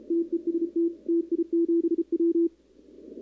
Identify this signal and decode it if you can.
Morse code, heard as a keyed tone, about 32 wpm, 340 Hz. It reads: TEST TI7W
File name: contests / 2019-cw-wpx-kt5j-k5pi- -2330 0-11